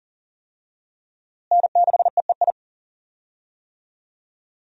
Morse code, heard as a keyed tone, about 40 wpm, 700 Hz. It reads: N6EEI
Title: USA Call Signs - 5 Characters - Repeated 2x - 40wpm